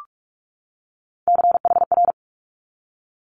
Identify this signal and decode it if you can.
Morse code, heard as a keyed tone, about 46 wpm, 700 Hz. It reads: CHR
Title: Multi-letter Phonemes 45wpm